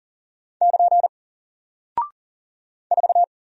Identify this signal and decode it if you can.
Morse code, heard as a keyed tone, about 41 wpm, 700 Hz. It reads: <KN> 4
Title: Single Character (Letter:Number:Punctuation:Prosign) Alt 40wpm